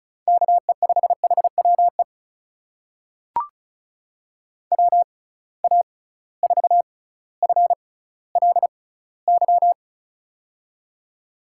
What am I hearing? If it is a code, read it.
Morse code, heard as a keyed tone, about 36 wpm, spaced as for 18 wpm, 700 Hz. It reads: KE5HWE WA4FLY
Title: USA Call Signs - 6 Characters - 6x Character Spacing - 35wpm